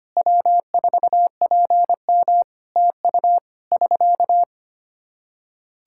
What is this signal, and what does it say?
Morse code, heard as a keyed tone, about 25 wpm, 700 Hz. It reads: W4PM TU <SK>
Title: Practice QSO - Exchange 4 - Ending - 25wpm